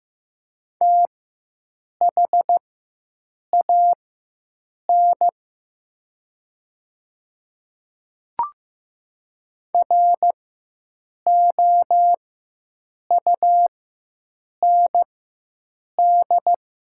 Morse code, heard as a keyed tone, about 15 wpm, spaced as for 7 wpm, 700 Hz. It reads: THAN ROUND
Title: 4x Character Spacing - Top 200 Words - 15wpm